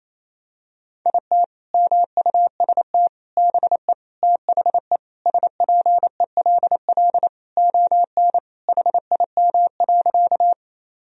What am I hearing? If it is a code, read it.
Morse code, heard as a keyed tone, about 27 wpm, 700 Hz. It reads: IT MUST BE THE SPELL ON HIM.
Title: Sentences from Top 1000 Words - Review Part 1 - 28wpm